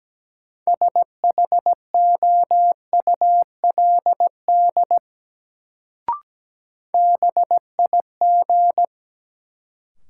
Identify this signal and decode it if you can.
Morse code, heard as a keyed tone, about 17 wpm, 700 Hz. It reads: SHOULD BIG
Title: Sets of 2 Words - New Word Drill - 17wpm